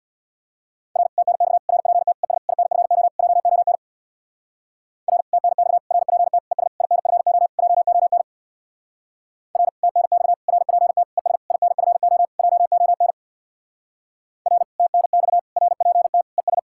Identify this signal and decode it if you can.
Morse code, heard as a keyed tone, about 66 wpm, 700 Hz. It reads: R TNX RPT ES INFO JON R TNX RPT ES INFO JON R TNX RPT ES INFO JON R TNX RPT ES